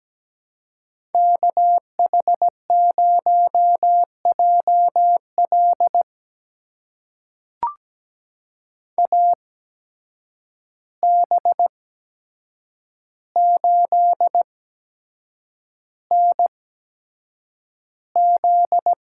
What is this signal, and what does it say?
Morse code, heard as a keyed tone, about 17 wpm, spaced as for 7 wpm, 700 Hz. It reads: KH0JL AB8NZ
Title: USA Call Signs - 5 Characters - 8x Character Spacing - 17wpm